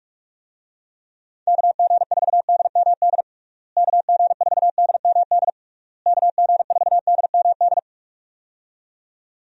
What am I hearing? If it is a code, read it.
Morse code, heard as a keyed tone, about 46 wpm, 700 Hz. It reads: KG4DMD KG4DMD KG4DMD
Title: USA Call Signs - 6 Characters - Repeated 3x - 45wpm